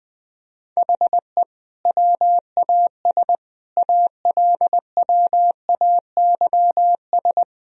Morse code, heard as a keyed tone, about 21 wpm, 700 Hz. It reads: HE WAS ALWAYS
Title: Sentences from Top 1000 Words - Review Part 4 - 20wpm